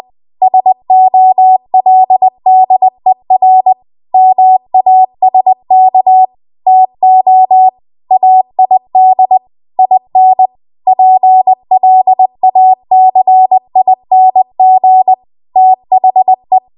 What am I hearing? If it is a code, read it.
Morse code, heard as a keyed tone, about 20 wpm, 750 Hz. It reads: SOLDER MASK TO AID IN PLACING THE
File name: Archive / 20 WPM / 20WPM